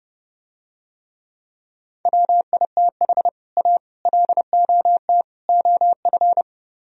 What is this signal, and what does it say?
Morse code, heard as a keyed tone, about 31 wpm, 700 Hz. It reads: WITH A LOT OF